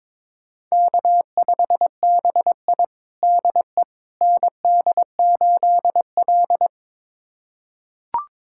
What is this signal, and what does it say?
Morse code, heard as a keyed tone, about 22 wpm, 700 Hz. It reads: K5BI DE ND8L
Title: USA Callsign DE Callsign - 4 Characters - 22wpm